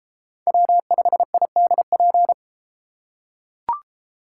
Morse code, heard as a keyed tone, about 32 wpm, 700 Hz. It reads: W5IDP